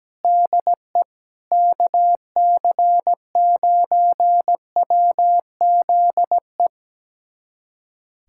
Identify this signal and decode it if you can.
Morse code, heard as a keyed tone, about 17 wpm, 700 Hz. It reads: DE KC9WZE
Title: USA Callsign DE Callsign - 6 Characters - 17wpm